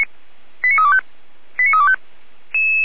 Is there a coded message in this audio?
Yes: five-tone selective call (ZVEI) from 0.6 to 2.0 s, heard as quick runs of five tones, tones of about 70 ms. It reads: 89325 89325